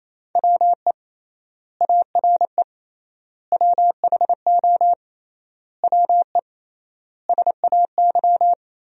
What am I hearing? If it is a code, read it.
Morse code, heard as a keyed tone, about 28 wpm, 700 Hz. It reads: WE ARE WHO WE SAY
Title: 3x Word-Spacing - Sentences from Top 200 Words - 28wpm